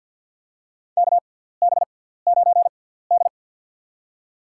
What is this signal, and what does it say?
Morse code, heard as a keyed tone, about 54 wpm, spaced as for 17 wpm, 700 Hz. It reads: KB9D
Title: USA Call Signs - 4 Characters - 6x Character Spacing - 50wpm